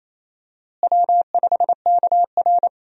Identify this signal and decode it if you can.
Morse code, heard as a keyed tone, about 28 wpm, 700 Hz. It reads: W5KR
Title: USA Call Signs - 4 Characters - Repeated 2x - 28wpm